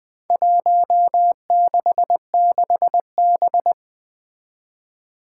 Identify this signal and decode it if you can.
Morse code, heard as a keyed tone, about 20 wpm, 700 Hz. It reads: 166B
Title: Four Challenging Letter-Number Combinations 20wpm